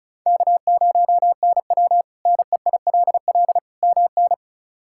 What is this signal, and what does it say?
Morse code, heard as a keyed tone, about 34 wpm, 700 Hz. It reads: K0NW NEILL MN
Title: K1USN SST QSO Practice - 35wpm